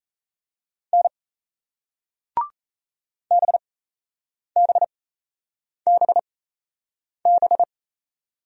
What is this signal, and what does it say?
Morse code, heard as a keyed tone, about 38 wpm, spaced as for 8 wpm, 700 Hz. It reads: N BBBB